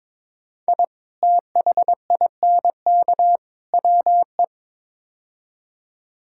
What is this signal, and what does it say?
Morse code, heard as a keyed tone, about 22 wpm, 700 Hz. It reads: I THINK WE
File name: practice / 22wpm / Build A Phrase - 5-Word Phrases - 22wpm